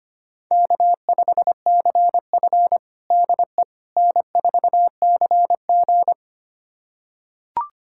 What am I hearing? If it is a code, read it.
Morse code, heard as a keyed tone, about 25 wpm, 700 Hz. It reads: K5CF DE N4CG
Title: USA Callsign DE Callsign - 4 Characters - 25wpm